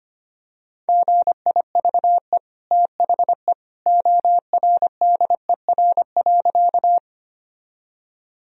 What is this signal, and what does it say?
Morse code, heard as a keyed tone, about 25 wpm, 700 Hz. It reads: GIVE THE ORDER.
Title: Sentences from Top 1000 Words - Review Part 2 - 25wpm